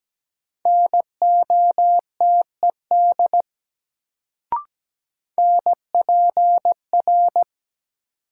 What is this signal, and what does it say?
Morse code, heard as a keyed tone, about 17 wpm, 700 Hz. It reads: NOTED NPR
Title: Sets of 5 Words - New Word Drill - 17wpm